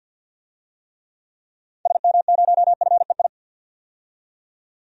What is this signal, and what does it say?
Morse code, heard as a keyed tone, about 53 wpm, 700 Hz. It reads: SM0FEI
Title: DXCC-100 Call Signs - 6 Characters - 50wpm